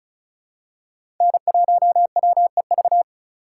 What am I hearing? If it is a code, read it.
Morse code, heard as a keyed tone, about 34 wpm, 700 Hz. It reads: N1WEV
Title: USA Call Signs - 5 Characters - 4x Character Spacing - 35wpm